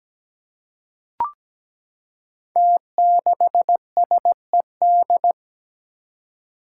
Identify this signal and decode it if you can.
Morse code, heard as a keyed tone, about 17 wpm, 700 Hz. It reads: T6SED
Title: Five Challenging Letter-Number Combinations 17wpm